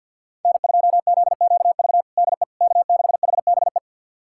Morse code, heard as a keyed tone, about 51 wpm, 700 Hz. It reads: N2ZQV DE K6HBE
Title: USA Callsign DE Callsign 50wpm